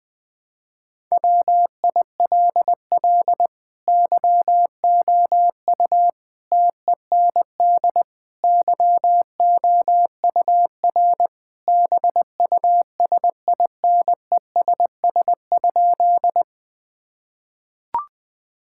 Morse code, heard as a keyed tone, about 20 wpm, 700 Hz. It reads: WILL YOU TEND YOUR BUSINESS?
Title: Sentences from Top 1000 Words - Review Part 5 - 20wpm